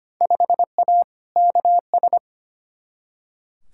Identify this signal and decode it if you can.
Morse code, heard as a keyed tone, about 25 wpm, 700 Hz. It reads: 5A KS
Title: Field Day Exchanges Encore - 25wpm